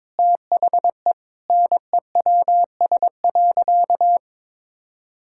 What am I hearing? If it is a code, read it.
Morse code, heard as a keyed tone, about 22 wpm, 700 Hz. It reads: THE NEWS.
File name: practice / 22wpm / Sentences with 3-Word Phrases - 22wpm